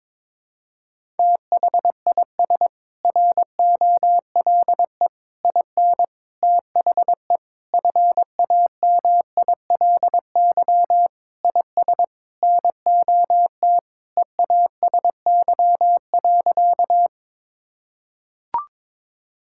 Morse code, heard as a keyed tone, about 22 wpm, 700 Hz. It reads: THIS ROLE IN THE FAMILY IS NOT EASY.
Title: Sentences from Top 1000 Words - Review Part 6 - 22wpm